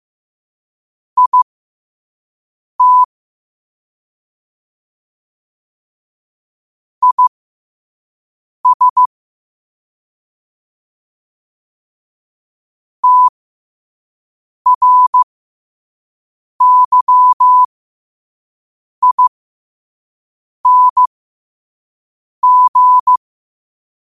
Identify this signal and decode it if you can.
Morse code, heard as a keyed tone, about 15 wpm, spaced as for 5 wpm, 1000 Hz. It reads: IT IS TRYING